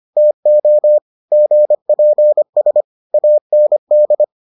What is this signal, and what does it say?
Morse code, heard as a keyed tone, about 25 wpm, 600 Hz. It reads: TO GPS AND